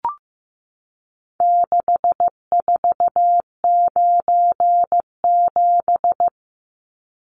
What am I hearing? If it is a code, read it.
Morse code, heard as a keyed tone, about 14 wpm, 700 Hz. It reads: 6497
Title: Numbers - 4-Digit 15wpm